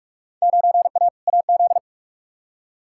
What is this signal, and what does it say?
Morse code, heard as a keyed tone, about 46 wpm, 700 Hz. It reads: 9A AZ
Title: Field Day Exchanges - 45wpm